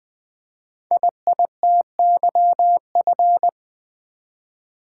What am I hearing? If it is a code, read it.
Morse code, heard as a keyed tone, about 20 wpm, 700 Hz. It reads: IITYF
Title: Code Groups - 5 Letters - 20wpm